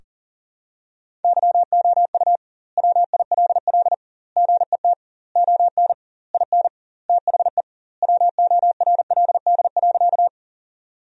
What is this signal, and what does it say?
Morse code, heard as a keyed tone, about 40 wpm, 700 Hz. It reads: YOU WILL GET ON IN THE WORLD.
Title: 2x Word-Spacing - Sentences from Top 200 Words - 40wpm